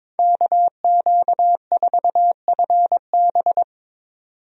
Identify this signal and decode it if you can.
Morse code, heard as a keyed tone, about 22 wpm, 700 Hz. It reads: KQ4FB
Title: USA Call Signs - 5 Characters - 22wpm